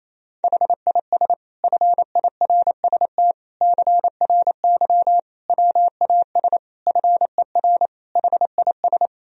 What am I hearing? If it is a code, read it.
Morse code, heard as a keyed tone, about 28 wpm, 700 Hz. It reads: HIS FIRST CRY WAS FER HIS
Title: Sentences from Top 500 Words and Abbreviations 28wpm